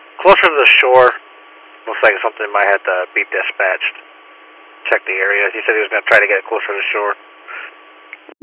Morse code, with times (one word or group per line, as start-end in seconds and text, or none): none